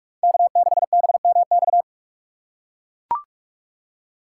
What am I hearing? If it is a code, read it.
Morse code, heard as a keyed tone, about 43 wpm, 700 Hz. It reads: K6BMX
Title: USA Call Signs - 4x Character Spacing - 45wpm